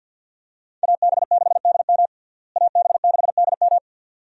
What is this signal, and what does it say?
Morse code, heard as a keyed tone, about 51 wpm, 700 Hz. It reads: AB6DM AB6DM
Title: USA Call Signs - 5 Characters - Repeated 2x - 50wpm